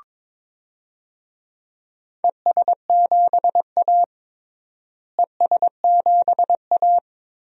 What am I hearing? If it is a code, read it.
Morse code, heard as a keyed tone, about 22 wpm, 700 Hz. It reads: ES7A ES7A
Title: DXCC-100 Call Signs - 4 Characters - Repeated 3x - 22wpm